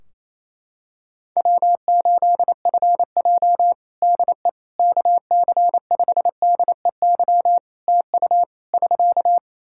Morse code, heard as a keyed tone, about 28 wpm, 700 Hz. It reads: W8FJ DE KC5DEY TU <SK>
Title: Practice QSO - Exchange 4 - Ending - 28wpm